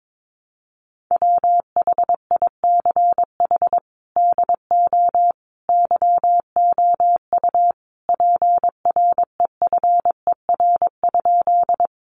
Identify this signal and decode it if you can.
Morse code, heard as a keyed tone, about 22 wpm, 700 Hz. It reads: WHICH DO YOU PREFER?
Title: Sentences from Top 1100 Words 22wpm